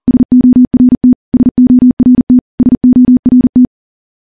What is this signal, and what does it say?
Morse code, heard as a keyed tone, about 40 wpm, 250 Hz. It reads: SORT SORT SORT
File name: morse / morse-training-40WPM